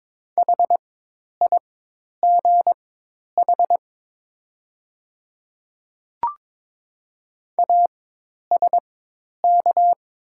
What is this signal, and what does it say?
Morse code, heard as a keyed tone, about 22 wpm, spaced as for 10 wpm, 700 Hz. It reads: HIGH ASK
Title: 4x Character Spacing - Top 200 Words - 22wpm